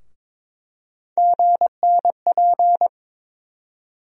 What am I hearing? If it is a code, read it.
Morse code, heard as a keyed tone, about 21 wpm, 700 Hz. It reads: GNP
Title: USA Call Sign Postfixes - 3 Letters 22wpm